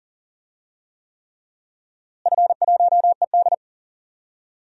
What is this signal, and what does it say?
Morse code, heard as a keyed tone, about 39 wpm, 700 Hz. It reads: F1ED
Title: DXCC-100 Call Signs - 4 Characters - Repeated 3x - 40wpm